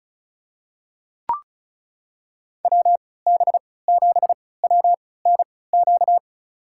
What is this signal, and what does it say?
Morse code, heard as a keyed tone, about 34 wpm, spaced as for 20 wpm, 690 Hz. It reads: WB7WNQ
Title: USA Call Signs - 6 Characters - 3x Character Spacing - 35wpm